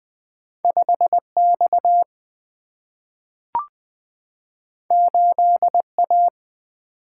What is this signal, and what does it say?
Morse code, heard as a keyed tone, about 20 wpm, 700 Hz. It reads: 5X 8A
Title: Single-Letter Single-Number 20wpm